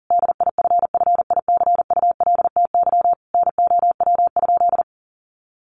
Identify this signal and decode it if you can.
Morse code, heard as a keyed tone, about 40 wpm, 700 Hz. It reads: DIFFICULTY NOW?
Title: Sentences from Top 1500 Words 40wpm